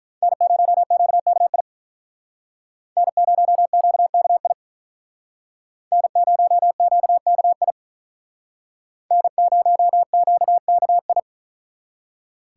Morse code, heard as a keyed tone, about 43 wpm, 700 Hz. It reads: N0QKI N0QKI N0QKI N0QKI
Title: Speed Racing - USA 1x3 Call Signs 35wpm